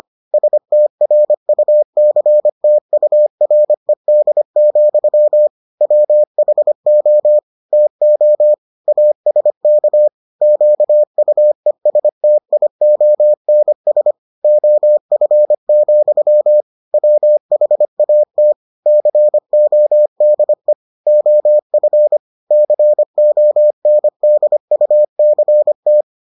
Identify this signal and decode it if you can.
Morse code, heard as a keyed tone, about 25 wpm, 600 Hz. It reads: STRUCTURED, WHO TO ASK QUESTIONS OF, WHAT CODE OF CONDUCT